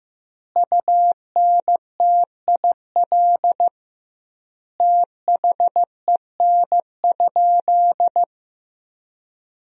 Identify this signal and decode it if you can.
Morse code, heard as a keyed tone, about 15 wpm, 700 Hz. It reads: UNTIL THEN?
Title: 2x Word-Spacing - Sentences from Top 500 Words - 15wpm